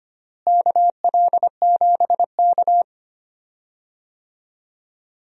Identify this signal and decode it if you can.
Morse code, heard as a keyed tone, about 25 wpm, 700 Hz. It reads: KL7K
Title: DXCC-100 Call Signs - 4 Characters - Repeated 2x - 25wpm